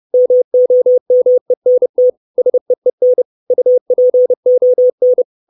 Morse code, heard as a keyed tone, about 30 wpm, 500 Hz. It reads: MOMENT SEEN UPON
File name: 632 Words 30WPM